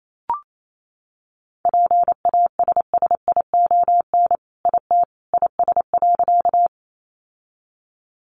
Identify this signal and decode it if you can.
Morse code, heard as a keyed tone, about 28 wpm, 700 Hz. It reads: PASSION IT IS.
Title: Sentences from Top 1300 Words 28wpm